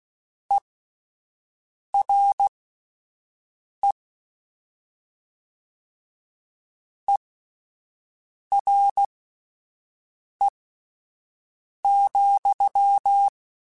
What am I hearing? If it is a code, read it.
Morse code, heard as a keyed tone, about 16 wpm, spaced as for 5 wpm, 800 Hz. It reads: ERE ERE,